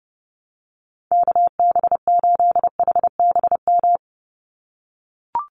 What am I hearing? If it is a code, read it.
Morse code, heard as a keyed tone, about 30 wpm, 700 Hz. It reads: KB8HBM
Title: USA Call Signs 30wpm